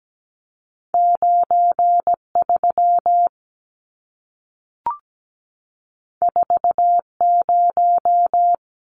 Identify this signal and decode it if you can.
Morse code, heard as a keyed tone, about 17 wpm, 700 Hz. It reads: 93 40
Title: Numbers - 2-Digit 17wpm